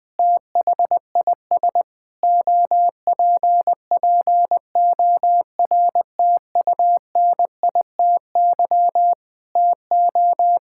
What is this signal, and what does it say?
Morse code, heard as a keyed tone, about 21 wpm, 700 Hz. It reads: THIS OPPORTUNITY TO
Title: Sentences with 8-Word Phrases - Spoken Beginning - 20wpm